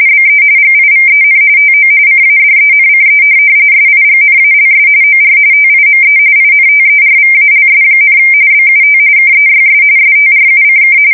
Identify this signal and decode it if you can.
RTTY, about 45.45 baud. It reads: BE GIVEN TO CADENCE SIGNALS OR TIME INTERVALS, BUT KEYS STRUCK
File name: fsk170 / morkrum